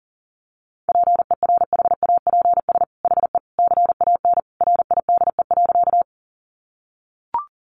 Morse code, heard as a keyed tone, about 37 wpm, 700 Hz. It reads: PERHAPS HE CAN RIDE.
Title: Sentences from Top 1200 Words 40wpm